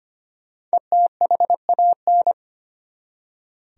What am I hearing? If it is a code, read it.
Morse code, heard as a keyed tone, about 25 wpm, 700 Hz. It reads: ETHAN
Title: USA Common Names - Max 5 Letters - 25wpm